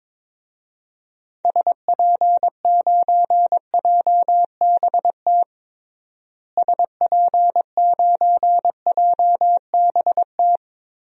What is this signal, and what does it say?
Morse code, heard as a keyed tone, about 22 wpm, 700 Hz. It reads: SP9JBT SP9JBT
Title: DXCC-100 Call Signs - 6 Characters - Repeated 3x - 22wpm